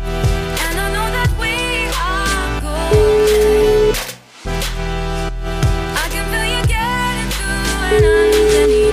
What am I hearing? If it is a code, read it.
Ringback tone (Europe, 425 Hz), with noise.